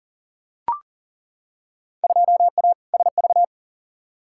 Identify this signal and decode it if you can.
Morse code, heard as a keyed tone, about 40 wpm, 700 Hz. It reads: 2A SV
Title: Field Day Exchanges - 40wpm